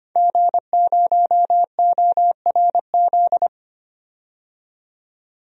Morse code, heard as a keyed tone, about 25 wpm, 700 Hz. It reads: G0ORZ
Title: DXCC-100 Call Signs - 5 Characters - 25wpm